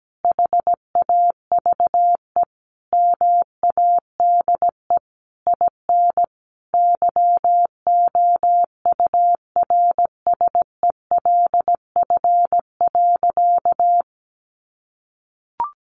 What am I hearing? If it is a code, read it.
Morse code, heard as a keyed tone, about 17 wpm, 700 Hz. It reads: HAVE MADE IN YOURSELF.
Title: Sentences from Top 600 Words 17wpm